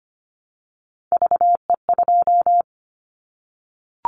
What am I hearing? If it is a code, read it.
Morse code, heard as a keyed tone, about 25 wpm, 700 Hz. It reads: VE2
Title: Callsign Prefixes for Top 50 DXCC - 25wpm